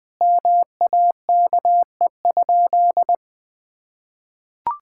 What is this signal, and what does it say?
Morse code, heard as a keyed tone, about 21 wpm, 700 Hz. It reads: MAKE?
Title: Sentences with 8-Word Phrases - Spoken Beginning - 20wpm